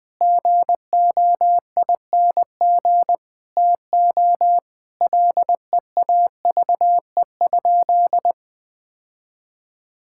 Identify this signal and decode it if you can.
Morse code, heard as a keyed tone, about 21 wpm, 700 Hz. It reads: GOING TO LEAVE?
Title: Sentences with 4-Word Phrases - 20wpm